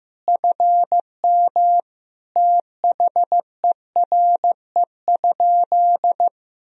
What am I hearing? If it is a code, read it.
Morse code, heard as a keyed tone, about 15 wpm, 700 Hz. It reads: FM THERE?
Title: Sentences from Top 100 Words and Abbreviations 15wpm